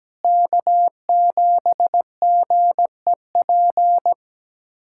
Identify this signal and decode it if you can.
Morse code, heard as a keyed tone, about 17 wpm, 700 Hz. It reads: K7GEP
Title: USA Callsign DE Callsign - 5 Characters - 17wpm